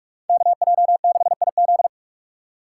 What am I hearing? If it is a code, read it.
Morse code, heard as a keyed tone, about 47 wpm, 700 Hz. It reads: KJ6IZ
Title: USA Call Signs - 5 Characters - 6x Character Spacing - 45wpm